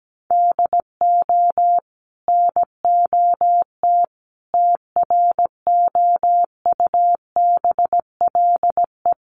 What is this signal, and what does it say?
Morse code, heard as a keyed tone, about 17 wpm, 700 Hz. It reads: DO NOT TROUBLE
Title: Sentences from Top 1200 Words 17wpm